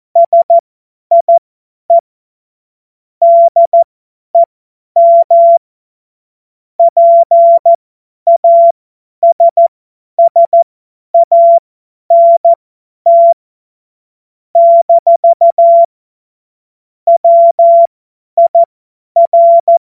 Morse code, heard as a keyed tone, about 14 wpm, spaced as for 10 wpm, 680 Hz. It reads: SIE DEM PASSANT - WIR
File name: Morsekurs / Witze / Witz 05 Dezember 2023 50 F70